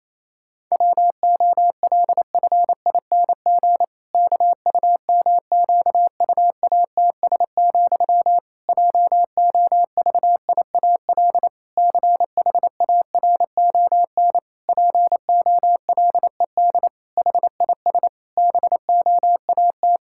Morse code, heard as a keyed tone, about 28 wpm, 700 Hz. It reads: WOLFING KUMQUATS, JOVIAL CHARON POLED HIS BOAT
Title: Pangram Challenge v2 - 28wpm